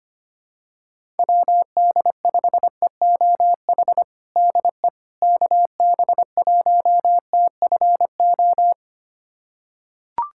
Morse code, heard as a keyed tone, about 25 wpm, 700 Hz. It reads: WD5EOH DE KB1TFO
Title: USA Callsign DE Callsign - 6 Characters - 25wpm